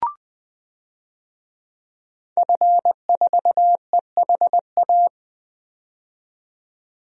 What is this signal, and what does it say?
Morse code, heard as a keyed tone, about 20 wpm, 700 Hz. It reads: F4EHA
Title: DXCC-100 Call Signs - 5 Characters - 20wpm